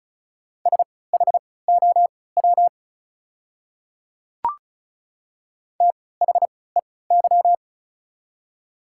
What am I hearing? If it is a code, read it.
Morse code, heard as a keyed tone, about 36 wpm, spaced as for 20 wpm, 700 Hz. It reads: SHOW THEY
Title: 3x Character Spacing - Top 200 Words - 35wpm